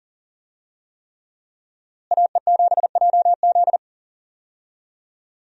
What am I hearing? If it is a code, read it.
Morse code, heard as a keyed tone, about 39 wpm, 700 Hz. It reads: AE7JZ
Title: USA Call Signs - 5 Characters - Repeated 3x - 40wpm